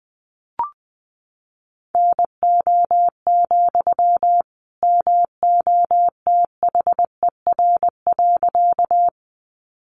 Morse code, heard as a keyed tone, about 20 wpm, 700 Hz. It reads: NO, MOTHER.
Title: Sentences from Top 200 Words 20wpm